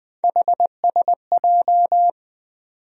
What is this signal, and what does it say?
Morse code, heard as a keyed tone, about 20 wpm, 700 Hz. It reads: HSJ
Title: USA Call Sign Postfixes - 3 Letters 20wpm